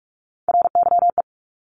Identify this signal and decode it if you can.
Morse code, heard as a keyed tone, about 46 wpm, 690 Hz. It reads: RYE
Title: Multi-letter Phonemes 45wpm